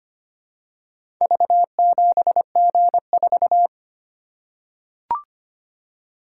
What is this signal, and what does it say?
Morse code, heard as a keyed tone, about 25 wpm, 700 Hz. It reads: V7G4
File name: practice / 25wpm / Four Letter-Number Combinations - 25wpm